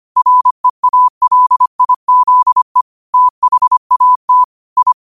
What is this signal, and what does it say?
Morse code, heard as a keyed tone, about 25 wpm, 1000 Hz. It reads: REALIZE THAT I